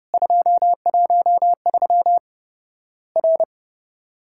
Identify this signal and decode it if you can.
Morse code, heard as a keyed tone, about 30 wpm, 700 Hz. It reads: 213 R
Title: Worked All Europe DX Contest - QTC Exchange - 30wpm